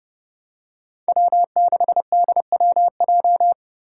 Morse code, heard as a keyed tone, about 30 wpm, 700 Hz. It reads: W6DWJ
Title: Five Challenging Letter-Number Combinations 30wpm